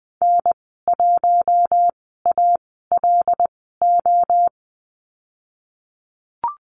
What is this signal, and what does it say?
Morse code, heard as a keyed tone, about 20 wpm, spaced as for 15 wpm, 700 Hz. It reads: N1ALO
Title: 076 Lesson 13g - Letter L - Call Signs 20wpm